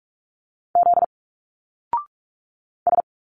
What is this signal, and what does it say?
Morse code, heard as a keyed tone, about 46 wpm, 700 Hz. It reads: Z S
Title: Single Letters 45wpm